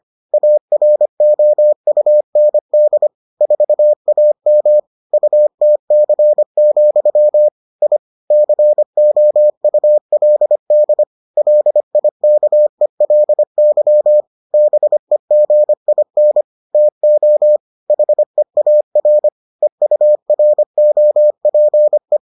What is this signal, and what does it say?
Morse code, heard as a keyed tone, about 25 wpm, 600 Hz. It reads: AROUND 4AM UTC, I COULD LIKELY BEGIN TO HEAR EUROPE